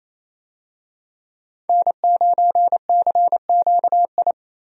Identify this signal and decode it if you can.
Morse code, heard as a keyed tone, about 28 wpm, 700 Hz. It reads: N9CQI
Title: USA Call Signs - 5 Characters - Repeated 2x - 28wpm